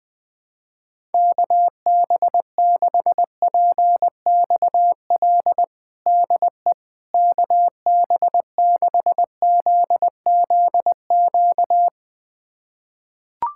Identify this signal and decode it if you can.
Morse code, heard as a keyed tone, about 20 wpm, 700 Hz. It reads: KB6PXL DE KB6ZZQ
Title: USA Callsign DE Callsign - 6 Characters - 20wpm